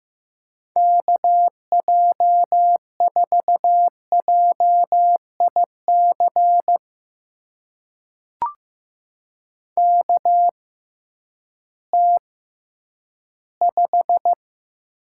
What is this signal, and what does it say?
Morse code, heard as a keyed tone, about 15 wpm, spaced as for 10 wpm, 700 Hz. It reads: KJ4JIC KT5
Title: USA Call Signs - 6x Character Spacing - 15wpm